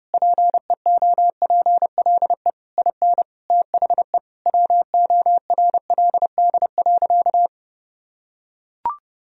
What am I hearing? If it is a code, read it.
Morse code, heard as a keyed tone, about 32 wpm, 700 Hz. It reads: PEOPLE IN THE WORLD.
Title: Sentences from Top 1000 Words - Review Part 4 - 30wpm